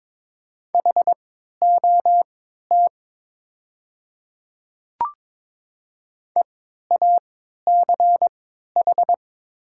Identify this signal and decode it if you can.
Morse code, heard as a keyed tone, about 22 wpm, spaced as for 13 wpm, 700 Hz. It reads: HOT EACH
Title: 3x Character Spacing - Top 100 Words - 22wpm